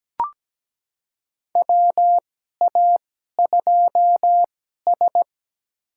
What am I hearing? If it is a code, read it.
Morse code, heard as a keyed tone, about 17 wpm, spaced as for 12 wpm, 700 Hz. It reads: WA2S
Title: USA Call Signs - 4 Characters - 2x Character Spacing - 17wpm